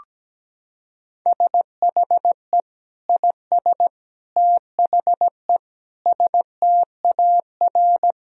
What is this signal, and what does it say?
Morse code, heard as a keyed tone, about 17 wpm, 700 Hz. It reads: SHE IS THE STAR
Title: Sentences from Top 1000 Words - Review Part 2 - 17wpm